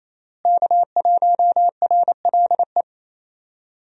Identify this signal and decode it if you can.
Morse code, heard as a keyed tone, about 28 wpm, 700 Hz. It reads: K1RLE
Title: USA Call Signs - 5 Characters - Repeated 3x - 28wpm